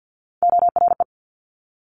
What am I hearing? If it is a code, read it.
Morse code, heard as a keyed tone, about 50 wpm, 700 Hz. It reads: ORE
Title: Multi-letter Phonemes Quick Fire 50wpm